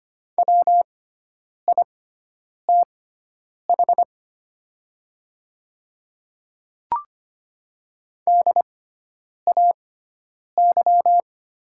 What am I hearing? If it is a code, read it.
Morse code, heard as a keyed tone, about 25 wpm, spaced as for 9 wpm, 690 Hz. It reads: WITH DAY
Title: 6x Character Spacing - Top 100 Words - 25wpm